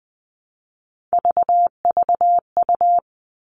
Morse code, heard as a keyed tone, about 20 wpm, 700 Hz. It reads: VVU
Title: All Cut Numbers - 3-Digit 20wpm